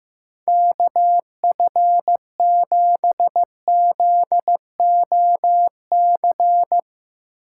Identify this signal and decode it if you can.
Morse code, heard as a keyed tone, about 15 wpm, 700 Hz. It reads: KF7ZOC
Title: Speed Racing - USA 2x3 Call Signs 15wpm